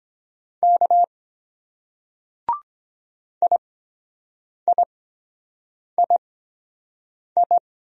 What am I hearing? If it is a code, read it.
Morse code, heard as a keyed tone, about 23 wpm, spaced as for 7 wpm, 700 Hz. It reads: K IIII